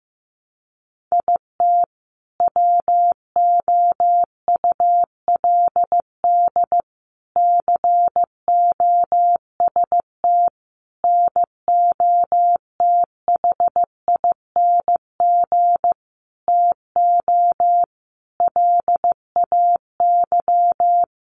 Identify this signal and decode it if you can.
Morse code, heard as a keyed tone, about 15 wpm, 700 Hz. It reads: IT WOULD COST NOTHING TO LAY
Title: Sentences from Top 800 Words 15wpm